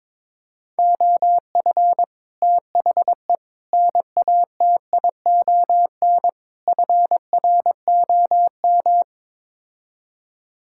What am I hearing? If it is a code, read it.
Morse code, heard as a keyed tone, about 22 wpm, 700 Hz. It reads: OF THE NATION FROM
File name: practice / 22wpm / Build A Phrase - 5-Word Phrases - 22wpm